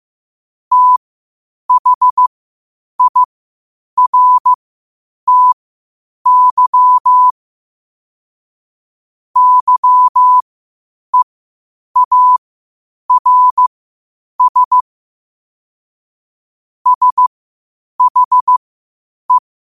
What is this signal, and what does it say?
Morse code, heard as a keyed tone, about 15 wpm, spaced as for 8 wpm, 1000 Hz. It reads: THIRTY YEARS SHE
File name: qotdfiles / 8wpm